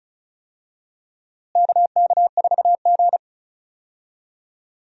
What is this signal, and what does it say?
Morse code, heard as a keyed tone, about 36 wpm, 700 Hz. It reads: KK4G
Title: USA Call Signs - 4 Characters - Repeated 3x - 35wpm